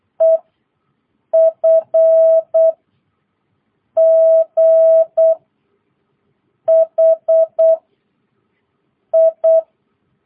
Morse code, heard as a keyed tone, about 8 wpm, 650 Hz. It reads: E F G H I
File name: jh8cbh / 2022cw